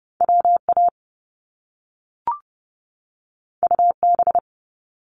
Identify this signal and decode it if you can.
Morse code, heard as a keyed tone, about 30 wpm, 700 Hz. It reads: W A U B
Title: Top Two Letter Combinations in Words 30wpm